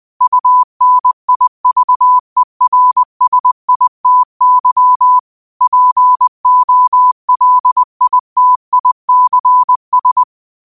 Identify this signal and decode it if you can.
Morse code, heard as a keyed tone, about 20 wpm, 1000 Hz. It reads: UNIVERSITY POLITICS